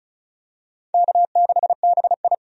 Morse code, heard as a keyed tone, about 36 wpm, 700 Hz. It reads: K6BI
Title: USA Call Signs - 4 Characters - 35wpm